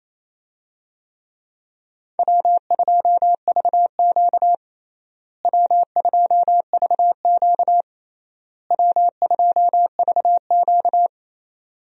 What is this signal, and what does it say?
Morse code, heard as a keyed tone, about 28 wpm, 700 Hz. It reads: W2VQ W2VQ W2VQ
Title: USA Call Signs - 4 Characters - Repeated 3x - 28wpm